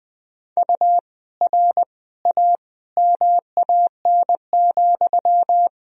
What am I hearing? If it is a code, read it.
Morse code, heard as a keyed tone, about 20 wpm, 700 Hz. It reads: U R A MAN,
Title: Sentences from Top 200 Words and Abbreviations 20wpm